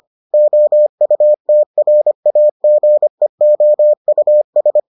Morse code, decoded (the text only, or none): OUTRAGEOUS